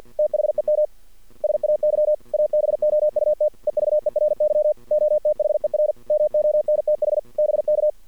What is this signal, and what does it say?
Morse code, heard as a keyed tone, about 48 wpm, 620 Hz. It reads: THEM SAY ABOUT EVERY OTHER MONTH BK
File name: archived / CRSnet